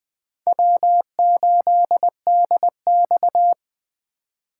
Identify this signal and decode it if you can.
Morse code, heard as a keyed tone, about 21 wpm, 700 Hz. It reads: W8DX